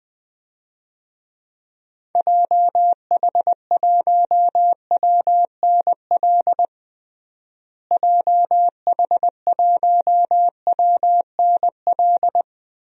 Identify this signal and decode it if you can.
Morse code, heard as a keyed tone, about 20 wpm, 700 Hz. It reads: JH1WNL JH1WNL
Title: DXCC-100 Call Signs - 6 Characters - Repeated 3x - 20wpm